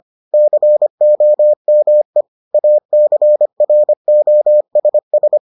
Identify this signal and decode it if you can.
Morse code, heard as a keyed tone, about 25 wpm, 600 Hz. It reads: COME ACROSS